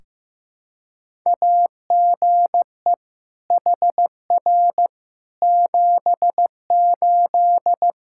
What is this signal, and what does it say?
Morse code, heard as a keyed tone, about 15 wpm, 700 Hz. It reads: AGE HR 78
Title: Practice QSO - Exchange 3 - More Info - 15wpm